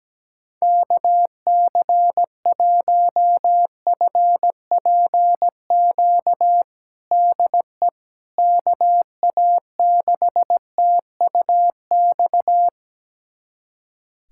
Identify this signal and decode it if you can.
Morse code, heard as a keyed tone, about 17 wpm, 700 Hz. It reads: KC1FPQ DE KA6TUX
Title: USA Callsign DE Callsign 17wpm